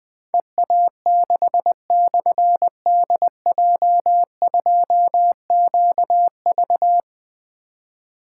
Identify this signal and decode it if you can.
Morse code, heard as a keyed tone, about 20 wpm, 700 Hz. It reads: EA6/DJ2QV
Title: Difficult Call Signs 20wpm